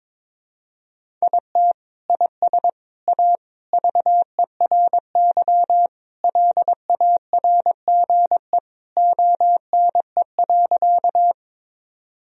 Morse code, heard as a keyed tone, about 22 wpm, 700 Hz. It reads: IT IS A VERY LARGE ONE.
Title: Sentences from Top 1000 Words - Review Part 1 - 22wpm